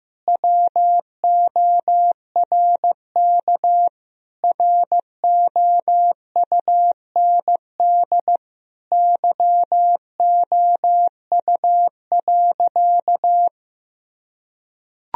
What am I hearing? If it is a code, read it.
Morse code, heard as a keyed tone, about 15 wpm, 700 Hz. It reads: WORK ROUND YOU.
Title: Sentences from Top 1000 Words - Review Part 1 - 15wpm